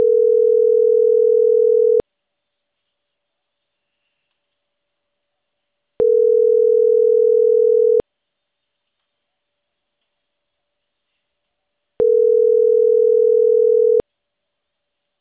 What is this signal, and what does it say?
Ringback tone (North America), 440 + 480 Hz.